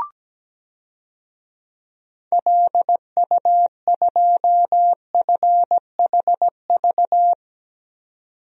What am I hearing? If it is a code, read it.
Morse code, heard as a keyed tone, about 17 wpm, 700 Hz. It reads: LU2FHV
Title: DXCC-100 Call Signs - 6 Characters - 17wpm